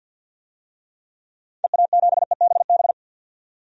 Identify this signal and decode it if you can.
Morse code, heard as a keyed tone, about 46 wpm, 700 Hz. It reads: EA7EBB
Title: DXCC-100 Call Signs - 6 Characters - 50wpm